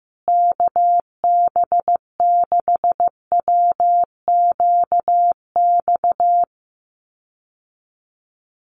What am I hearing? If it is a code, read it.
Morse code, heard as a keyed tone, about 15 wpm, 700 Hz. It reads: KB6WQX
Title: USA Call Signs Repeated 3x 15wpm